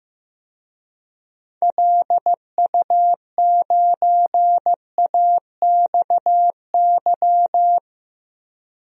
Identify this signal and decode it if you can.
Morse code, heard as a keyed tone, about 15 wpm, 700 Hz. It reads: LU9AXY
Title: DXCC-100 Call Signs - 6 Characters - 15wpm